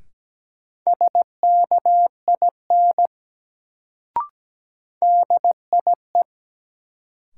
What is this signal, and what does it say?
Morse code, heard as a keyed tone, about 17 wpm, 700 Hz. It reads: SKIN DIE
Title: Sentences from Top 1300 Words - New Word Drill - 17wpm